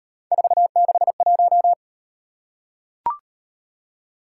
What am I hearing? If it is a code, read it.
Morse code, heard as a keyed tone, about 36 wpm, 700 Hz. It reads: 461